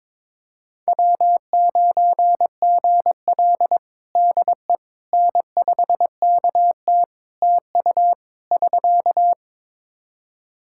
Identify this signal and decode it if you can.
Morse code, heard as a keyed tone, about 22 wpm, 700 Hz. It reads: W9GL DE N5KT TU <SK>
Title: Practice QSO - Exchange 4 - Ending - 22wpm